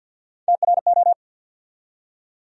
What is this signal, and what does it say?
Morse code, heard as a keyed tone, about 51 wpm, 700 Hz. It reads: TRO
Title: USA Call Sign Postfixes - 3 Letters 50wpm